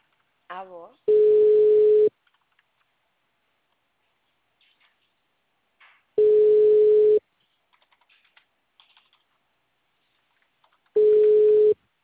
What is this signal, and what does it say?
Ringback tone (Europe, 425 Hz), with speech.
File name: out- -2008- - - .86276